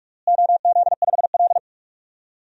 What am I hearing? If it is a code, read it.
Morse code, heard as a keyed tone, about 46 wpm, 700 Hz. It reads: OZ5L
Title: DXCC-100 Call Signs - 4 Characters - Repeated 3x - 45wpm